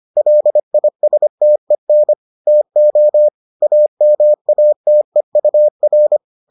Morse code, heard as a keyed tone, about 25 wpm, 600 Hz. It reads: LISTEN TO AMATEUR